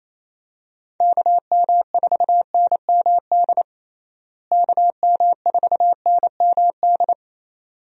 Morse code, heard as a keyed tone, about 28 wpm, 700 Hz. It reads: KM4NMD KM4NMD
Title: USA Call Signs - 6 Characters - Repeated 2x - 28wpm